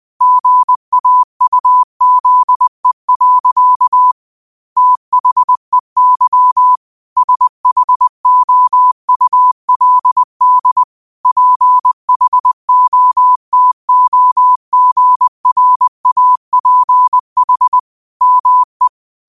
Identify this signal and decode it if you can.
Morse code, heard as a keyed tone, about 20 wpm, 1000 Hz. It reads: GAUZE. THEY SHOULD PHOTOGRAPH ME